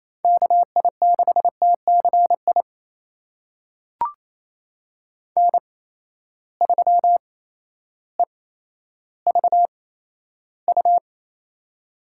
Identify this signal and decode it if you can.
Morse code, heard as a keyed tone, about 28 wpm, spaced as for 13 wpm, 700 Hz. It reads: KI6TCI N3EVU
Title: USA Call Signs - 8x Character Spacing - 28wpm